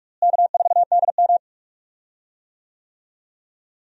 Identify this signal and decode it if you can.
Morse code, heard as a keyed tone, about 46 wpm, 700 Hz. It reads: K4DM
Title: USA Call Signs - 4 Characters - Repeated 2x - 45wpm